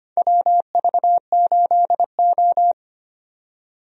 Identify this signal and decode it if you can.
Morse code, heard as a keyed tone, about 25 wpm, 700 Hz. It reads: WV8O
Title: USA Call Signs - 4 Characters - 25wpm